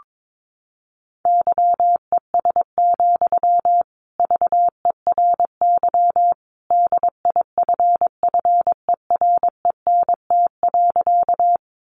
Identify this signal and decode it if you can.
Morse code, heard as a keyed tone, about 22 wpm, 700 Hz. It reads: YES, VERY DIFFERENT.
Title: Sentences from Top 600 Words 22wpm